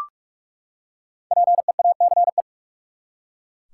Morse code, heard as a keyed tone, about 42 wpm, 700 Hz. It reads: PEACE
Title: Five Letter Words - 45wpm